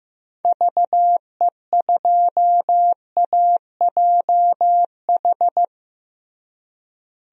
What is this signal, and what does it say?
Morse code, heard as a keyed tone, about 16 wpm, 700 Hz. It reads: VE2AJH
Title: DXCC-100 Call Signs - 6 Characters - Repeated 2x - 15wpm